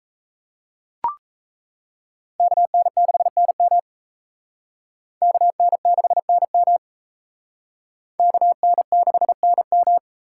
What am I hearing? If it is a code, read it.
Morse code, heard as a keyed tone, about 36 wpm, 700 Hz. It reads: KN6NM KN6NM KN6NM